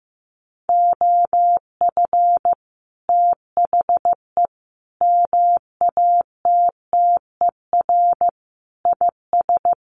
Morse code, heard as a keyed tone, about 15 wpm, 700 Hz. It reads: OF THE MATTER IS
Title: Sets of 5 Words Encore 15wpm